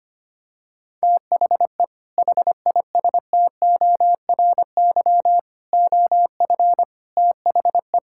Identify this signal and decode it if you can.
Morse code, heard as a keyed tone, about 25 wpm, 700 Hz. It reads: THE HISTORY OF THE
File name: practice / 25wpm / Sentences with 3-Word Phrases - 25wpm